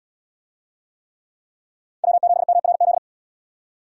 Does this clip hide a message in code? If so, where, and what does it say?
Yes: Morse code from 2.0 to 3.0 s, heard as a keyed tone, about 68 wpm, 700 Hz. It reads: W7MRZ